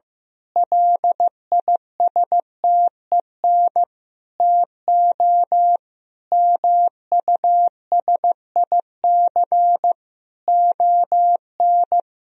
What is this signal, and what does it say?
Morse code, heard as a keyed tone, about 15 wpm, 700 Hz. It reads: LISTEN TO MUSIC ON
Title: Sentences from Top 1000 Words - Review Part 3 - 15wpm